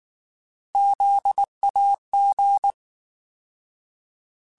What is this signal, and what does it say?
Morse code, heard as a keyed tone, about 19 wpm, 800 Hz. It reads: ZAG
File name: Alphabetical / Alphabetical mp3s / Zs @18